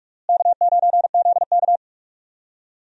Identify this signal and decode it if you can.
Morse code, heard as a keyed tone, about 43 wpm, 700 Hz. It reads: K9ZK
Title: USA Call Signs - 4 Characters - Repeated 2x - 45wpm